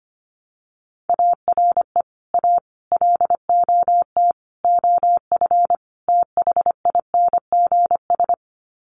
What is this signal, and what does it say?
Morse code, heard as a keyed tone, about 25 wpm, 700 Hz. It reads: ARE A LOT OF THINGS
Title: Sets of 5 Words Encore2 25wpm